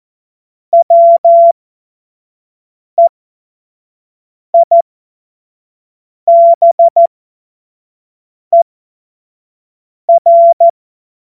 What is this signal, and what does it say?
Morse code, heard as a keyed tone, about 14 wpm, spaced as for 5 wpm, 680 Hz. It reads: WEIBER
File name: Morsekurs / Witze / Witz 25. Januar 2022, 25 F70